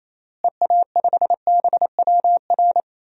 Morse code, heard as a keyed tone, about 28 wpm, 700 Hz. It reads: EA5BWR
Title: DXCC-100 Call Signs - 6 Characters - Repeated 3x - 28wpm